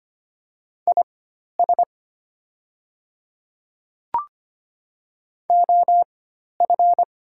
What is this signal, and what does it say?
Morse code, heard as a keyed tone, about 25 wpm, spaced as for 12 wpm, 700 Hz. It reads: IS OF
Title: 4x Character Spacing - Top 100 Words - 25wpm